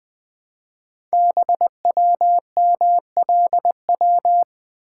Morse code, heard as a keyed tone, about 20 wpm, 700 Hz. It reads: BWMLW
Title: Code Groups - 5 Letters - 20wpm